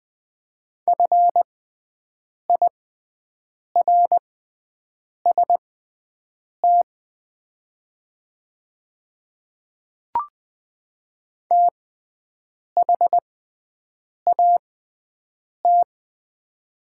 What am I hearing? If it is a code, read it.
Morse code, heard as a keyed tone, about 20 wpm, spaced as for 7 wpm, 690 Hz. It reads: FIRST THAT
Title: 6x Character Spacing - Top 100 Words - 20wpm